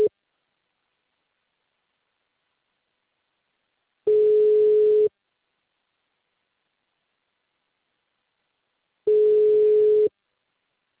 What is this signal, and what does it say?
Ringback tone (Europe, 425 Hz).